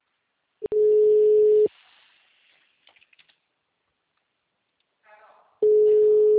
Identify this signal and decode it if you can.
Ringback tone (Europe, 425 Hz), with speech.